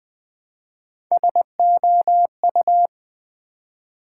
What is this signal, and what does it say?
Morse code, heard as a keyed tone, about 20 wpm, 700 Hz. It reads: SOU
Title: Build A Word - 1 Letter At A Time - 4-Letter Words - 20wpm